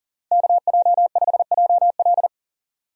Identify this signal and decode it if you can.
Morse code, heard as a keyed tone, about 37 wpm, 700 Hz. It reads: KJ5JL
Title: USA Call Signs - 5 Characters - Repeated 3x - 40wpm